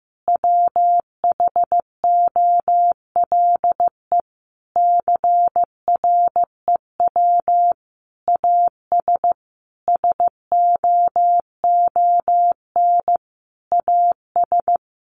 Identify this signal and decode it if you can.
Morse code, heard as a keyed tone, about 15 wpm, 700 Hz. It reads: WHOLE CREW AS SOON AS
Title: Sentences from Top 1500 Words 15wpm